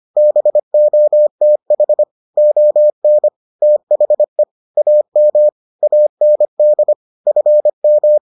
Morse code, heard as a keyed tone, about 25 wpm, 600 Hz. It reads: BOTH ON THE AM AND FM